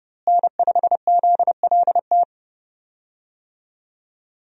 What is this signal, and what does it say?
Morse code, heard as a keyed tone, about 30 wpm, 700 Hz. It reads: N5ZLT
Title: USA Call Signs - 5 Characters - Repeated 2x - 30wpm